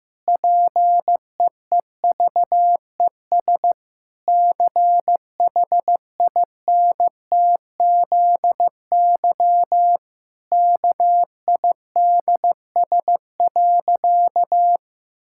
Morse code, heard as a keyed tone, about 14 wpm, 700 Hz. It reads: PEEVES CHINTZY KIDS.
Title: Pangram Challenge v2 - 15wpm